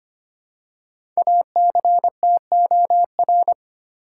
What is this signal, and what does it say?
Morse code, heard as a keyed tone, about 25 wpm, 700 Hz. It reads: ACTOR
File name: 5 Letter Words - First 3 Letters Given - 25wpm